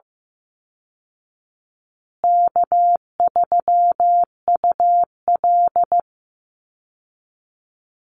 Morse code, heard as a keyed tone, about 15 wpm, 700 Hz. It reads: K3UL
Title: Easy Call Signs Repeated 3x - 15wpm